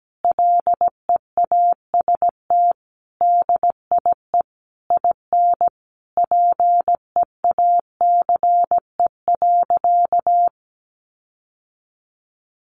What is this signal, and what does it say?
Morse code, heard as a keyed tone, about 17 wpm, 700 Hz. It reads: LEAST DIE IN PEACE.
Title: Sentences from Top 1500 Words 17wpm